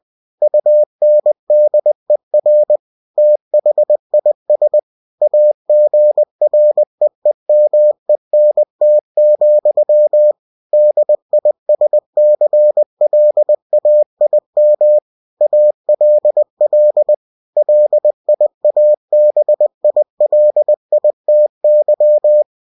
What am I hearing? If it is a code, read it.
Morse code, heard as a keyed tone, about 20 wpm, 600 Hz. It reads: UNDER THIS AGREEMENT, DISCLAIM ALL LIABILITY